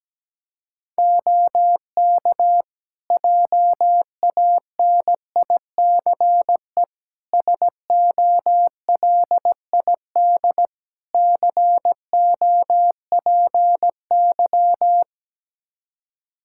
Morse code, heard as a keyed tone, about 17 wpm, 700 Hz. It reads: OK JANICE SOLID COPY
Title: Practice QSO - Exchange 3 - More Info - 17wpm